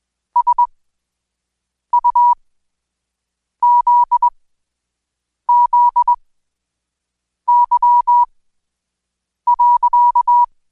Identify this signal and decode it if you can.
Morse code, heard as a keyed tone, about 20 wpm, spaced as for 6 wpm, 970 Hz. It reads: SUZZY.
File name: Code Quick/AudioDownld2/Seg109-Day14-DayDrill 14